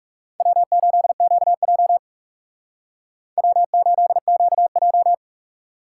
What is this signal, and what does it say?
Morse code, heard as a keyed tone, about 48 wpm, 700 Hz. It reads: W8QJ W8QJ